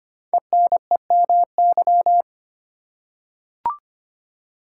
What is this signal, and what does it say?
Morse code, heard as a keyed tone, about 25 wpm, 700 Hz. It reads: ENEMY
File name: 5 Letter Words - First 3 Letters Given - 25wpm